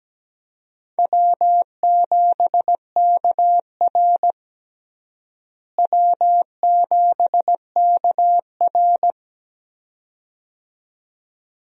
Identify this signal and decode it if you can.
Morse code, heard as a keyed tone, about 17 wpm, 700 Hz. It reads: W7KR W7KR
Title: USA Call Signs - 4 Characters - Repeated 2x - 17wpm